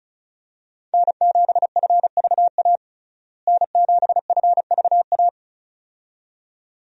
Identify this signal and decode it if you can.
Morse code, heard as a keyed tone, about 36 wpm, 700 Hz. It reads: N7FVA N7FVA
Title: USA Call Signs - 5 Characters - Repeated 2x - 35wpm